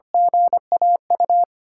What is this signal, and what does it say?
Morse code, heard as a keyed tone, about 25 wpm, 700 Hz. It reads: GAU
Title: USA Call Signs - 6 Characters - 3x Character Spacing - 25wpm